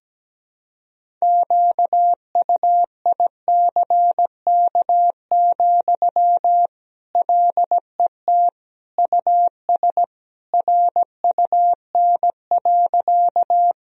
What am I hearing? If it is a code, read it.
Morse code, heard as a keyed tone, about 17 wpm, 700 Hz. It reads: QUICK, LET US RUN.
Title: Sentences from Top 1000 Words - Review Part 1 - 17wpm